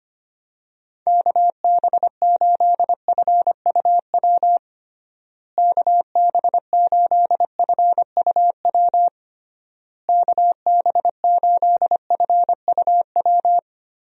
Morse code, heard as a keyed tone, about 25 wpm, 700 Hz. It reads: KB8FUW KB8FUW KB8FUW
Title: USA Call Signs - 6 Characters - Repeated 3x - 25wpm